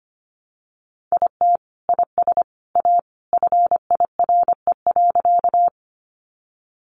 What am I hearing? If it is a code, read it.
Morse code, heard as a keyed tone, about 25 wpm, 700 Hz. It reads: IT IS A FIRE.
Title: Sentences from Top 400 Words 25wpm